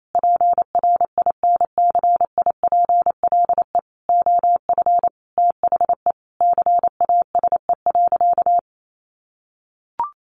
Morse code, heard as a keyed tone, about 28 wpm, 700 Hz. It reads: PRINCIPLE OF THE CASE.
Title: Sentences from Top 1400 Words 28wpm